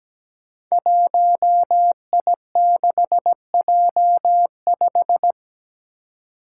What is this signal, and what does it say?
Morse code, heard as a keyed tone, about 17 wpm, 700 Hz. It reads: 1I6J5
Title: Five Challenging Letter-Number Combinations 17wpm